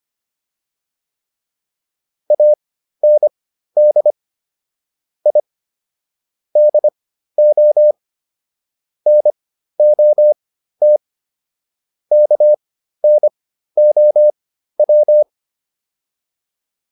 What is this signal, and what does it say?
Morse code, heard as a keyed tone, about 25 wpm, spaced as for 13 wpm, 600 Hz. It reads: AND I DO NOT KNOW